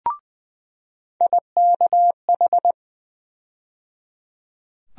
Morse code, heard as a keyed tone, about 20 wpm, 700 Hz. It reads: IKH
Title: USA Call Sign Postfixes - 3 Letters 20wpm